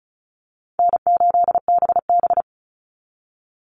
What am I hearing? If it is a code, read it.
Morse code, heard as a keyed tone, about 34 wpm, 700 Hz. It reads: N8BB
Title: NA SOTA Chaser Callsigns 35wpm